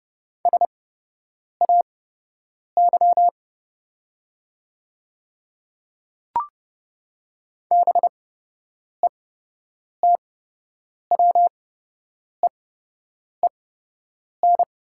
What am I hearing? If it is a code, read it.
Morse code, heard as a keyed tone, about 29 wpm, spaced as for 8 wpm, 700 Hz. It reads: SAY BETWEEN
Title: 8x Character Spacing - Top 300 Words - 30wpm